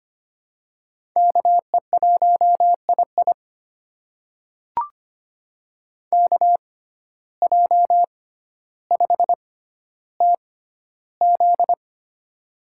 Morse code, heard as a keyed tone, about 25 wpm, spaced as for 13 wpm, 700 Hz. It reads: KE1II KJ5TZ
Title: USA Call Signs - 5 Characters - 6x Character Spacing - 25wpm